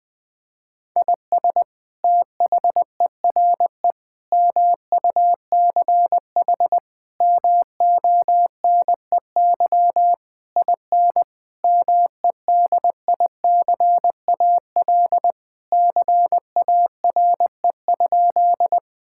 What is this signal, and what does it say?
Morse code, heard as a keyed tone, about 20 wpm, 700 Hz. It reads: IS THERE MUCH MONEY IN MEDICAL CARE?
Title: Sentences from Top 1000 Words - Review Part 5 - 20wpm